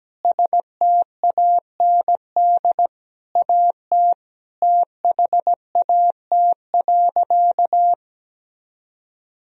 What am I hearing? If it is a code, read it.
Morse code, heard as a keyed tone, about 17 wpm, 700 Hz. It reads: STAND AT THAT.
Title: Sentences from Top 1000 Words - Review Part 1 - 17wpm